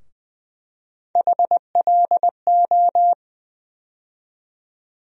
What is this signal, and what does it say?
Morse code, heard as a keyed tone, about 20 wpm, 700 Hz. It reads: HLO
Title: USA Call Sign Postfixes - 3 Letters 20wpm